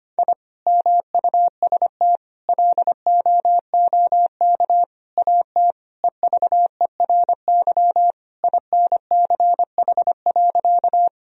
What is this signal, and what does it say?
Morse code, heard as a keyed tone, about 25 wpm, 700 Hz. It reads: I MUST LOOK AT EVERY INCH.
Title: Sentences from Top 1000 Words - Review Part 3 - 25wpm